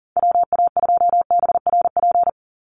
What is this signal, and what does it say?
Morse code, heard as a keyed tone, about 40 wpm, 690 Hz. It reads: WA2BRP
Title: USA Call Signs 40wpm